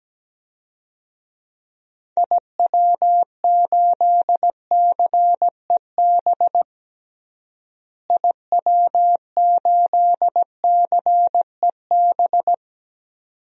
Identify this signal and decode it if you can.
Morse code, heard as a keyed tone, about 17 wpm, 700 Hz. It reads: IW8CEB IW8CEB
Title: DXCC-100 Call Signs - 6 Characters - Repeated 2x - 17wpm